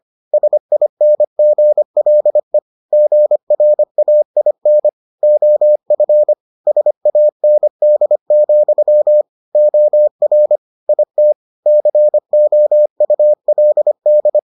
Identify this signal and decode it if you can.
Morse code, heard as a keyed tone, about 25 wpm, 600 Hz. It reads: SINGLE GRAIN OF SAND, OR IT COULD